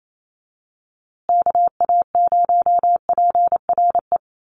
Morse code, heard as a keyed tone, about 28 wpm, 700 Hz. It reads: KA0PRE
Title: USA Call Signs Repeated 3x 28wpm